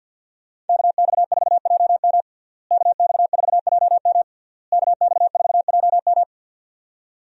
Morse code, heard as a keyed tone, about 47 wpm, 700 Hz. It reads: KX4JM KX4JM KX4JM
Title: USA Call Signs - 5 Characters - Repeated 3x - 50wpm